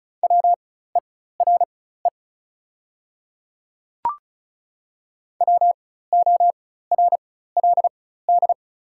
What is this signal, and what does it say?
Morse code, heard as a keyed tone, about 36 wpm, spaced as for 16 wpm, 700 Hz. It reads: WERE WORLD
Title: 4x Character Spacing - Top 200 Words - 35wpm